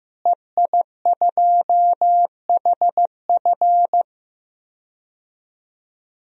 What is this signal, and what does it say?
Morse code, heard as a keyed tone, about 15 wpm, 700 Hz. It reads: EI2HF
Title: DXCC-100 Call Signs - 5 Characters - Repeated 2x - 15wpm